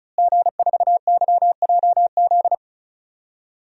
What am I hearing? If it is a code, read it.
Morse code, heard as a keyed tone, about 34 wpm, 700 Hz. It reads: G4YJZ
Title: DXCC-100 Call Signs - 5 Characters - Repeated 3x - 35wpm